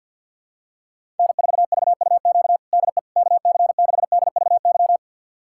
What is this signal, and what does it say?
Morse code, heard as a keyed tone, about 51 wpm, 700 Hz. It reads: N4VUQ DE KC6DVY
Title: USA Callsign DE Callsign 50wpm